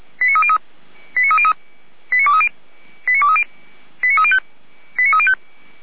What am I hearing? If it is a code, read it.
Five-tone selective call (ZVEI), heard as quick runs of five tones, tones of about 70 ms. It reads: 89303 89303 89230 89230 89305 89305